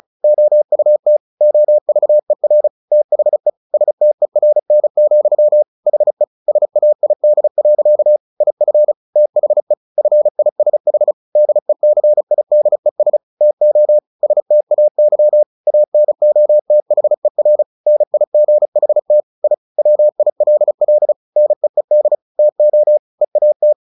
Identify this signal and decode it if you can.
Morse code, heard as a keyed tone, about 34 wpm, 600 Hz. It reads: OUT OVER THE STERN, HE SAID. IF THE FISH DECIDES TO STAY ANOTHER NIGHT I WILL NEED TO EAT
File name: the old man and the sea / the old man and the sea 1 35wpm